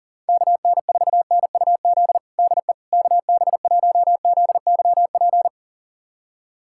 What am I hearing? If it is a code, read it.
Morse code, heard as a keyed tone, about 40 wpm, 700 Hz. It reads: KN4NUZ DE KB1ZYP
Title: USA Callsign DE Callsign 40wpm